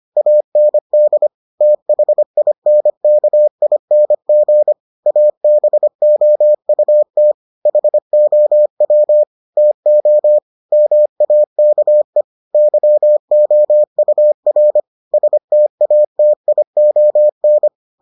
Morse code, heard as a keyed tone, about 25 wpm, 600 Hz. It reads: AND THINKING ABOUT HOW TO MAKE YOUR STATION